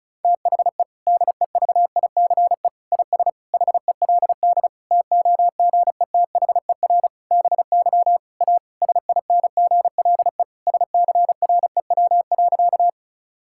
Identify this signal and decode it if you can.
Morse code, heard as a keyed tone, about 34 wpm, 700 Hz. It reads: THE DEVICE IS HELD TOGETHER BY A SINGLE SCREW.
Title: Sentences from Top 1600 Words 35wpm